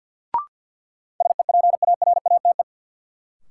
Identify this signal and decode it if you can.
Morse code, heard as a keyed tone, about 49 wpm, 700 Hz. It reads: SEPARATE
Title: Sentences from Top 1000 Words - Review Part 7 - New Word Drill - 50wpm